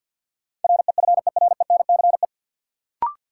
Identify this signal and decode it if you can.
Morse code, heard as a keyed tone, about 51 wpm, 700 Hz. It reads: REFERENCE
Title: Sentences from Top 1200 Words - New Word Drill Alt - 50wpm